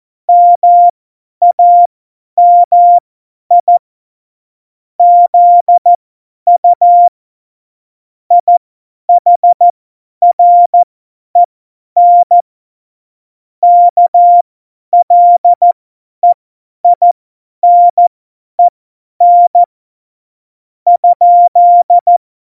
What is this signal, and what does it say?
Morse code, heard as a keyed tone, about 14 wpm, spaced as for 10 wpm, 700 Hz. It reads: MAMI ZU IHREN KLEINEN ?
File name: Morsekurs / Witze / Witz 19. Juni 2018, 50 F70